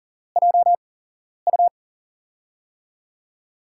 Morse code, heard as a keyed tone, about 40 wpm, spaced as for 11 wpm, 700 Hz. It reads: JU
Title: USA Call Signs - 8x Character Spacing - 40wpm